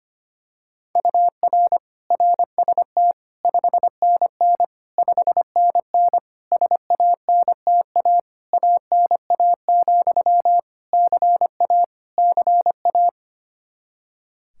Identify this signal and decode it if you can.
Morse code, heard as a keyed tone, about 25 wpm, 700 Hz. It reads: UR RST 5NN 5NN SANTA ANA, CA CA
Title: Practice QSO - Exchange 1 - Beginning - 25wpm